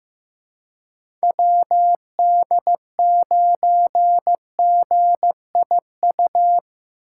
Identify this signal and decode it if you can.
Morse code, heard as a keyed tone, about 18 wpm, 700 Hz. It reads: WD9GIU
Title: Speed Racing - USA 2x3 Call Signs 15wpm